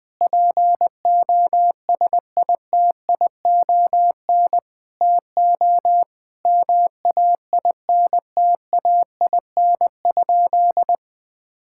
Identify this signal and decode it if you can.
Morse code, heard as a keyed tone, about 21 wpm, 700 Hz. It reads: POSITION TO MAINTAIN?
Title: Sentences from Top 1000 Words - Review Part 5 - 20wpm